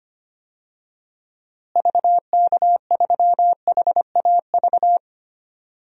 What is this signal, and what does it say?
Morse code, heard as a keyed tone, about 25 wpm, 700 Hz. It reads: VK3HAV
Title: DXCC-100 Call Signs - 6 Characters - Repeated 2x - 25wpm